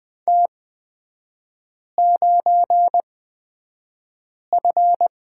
Morse code, heard as a keyed tone, about 20 wpm, spaced as for 5 wpm, 700 Hz. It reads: T9F
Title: 224 Lesson 36e - BT - Three Characters 20wpm